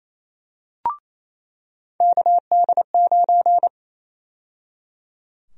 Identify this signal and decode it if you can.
Morse code, heard as a keyed tone, about 28 wpm, 700 Hz. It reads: KD9
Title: USA Call Sign Prefixes - 28wpm